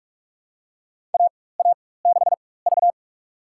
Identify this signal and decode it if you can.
Morse code, heard as a keyed tone, about 46 wpm, spaced as for 21 wpm, 700 Hz. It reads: AA6V
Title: USA Call Signs - 4 Characters - 4x Character Spacing - 45wpm